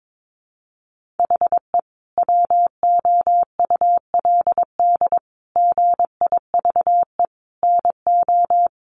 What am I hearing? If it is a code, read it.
Morse code, heard as a keyed tone, about 22 wpm, 700 Hz. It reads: HE WOULD GIVE NO
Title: Sentences from Top 1200 Words 22wpm